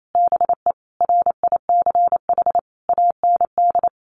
Morse code, heard as a keyed tone, about 28 wpm, 700 Hz. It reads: BE RICH AND
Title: Sentences from Top 1100 Words 28wpm